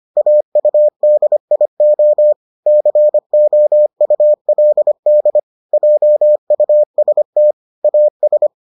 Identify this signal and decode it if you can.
Morse code, heard as a keyed tone, about 25 wpm, 600 Hz. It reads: AUDIO COULD JUST AS